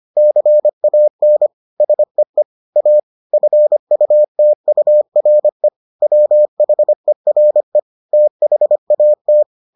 Morse code, heard as a keyed tone, about 25 wpm, 600 Hz. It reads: CAN SEE A FUTURE WHERE THAT